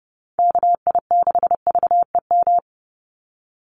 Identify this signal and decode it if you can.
Morse code, heard as a keyed tone, about 30 wpm, 700 Hz. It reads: KI6VEM
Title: USA Call Signs 30wpm